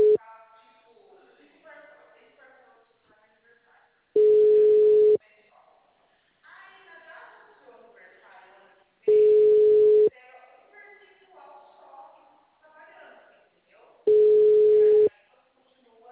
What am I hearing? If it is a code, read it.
Ringback tone (Europe, 425 Hz), with speech.